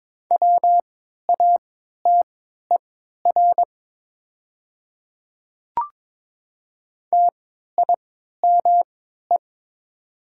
Morse code, heard as a keyed tone, about 22 wpm, spaced as for 13 wpm, 690 Hz. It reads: WATER TIME
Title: 3x Character Spacing - Top 100 Words - 22wpm